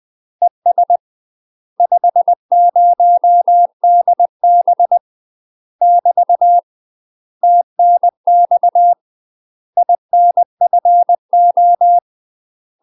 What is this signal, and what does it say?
Morse code, heard as a keyed tone, about 20 wpm, 700 Hz. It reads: ES 50DB <BT> TNX INFO
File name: UK MASTER QSOs TRAINING/20WPM